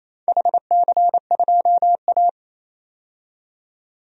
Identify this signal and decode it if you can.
Morse code, heard as a keyed tone, about 28 wpm, 700 Hz. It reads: HC2A
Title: DXCC-100 Call Signs - 4 Characters - 28wpm